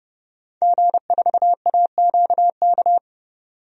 Morse code, heard as a keyed tone, about 29 wpm, 700 Hz. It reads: G4AQK
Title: DXCC-100 Call Signs - 5 Characters - 30wpm